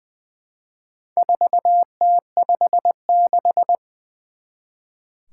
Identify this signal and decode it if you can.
Morse code, heard as a keyed tone, about 20 wpm, 700 Hz. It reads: 4T56
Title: Four Challenging Letter-Number Combinations 20wpm